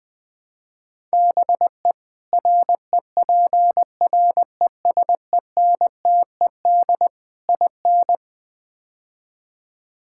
Morse code, heard as a keyed tone, about 21 wpm, 700 Hz. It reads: BE REPRESENTED IN
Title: Build A Phrase - 5-Word Phrases - 20wpm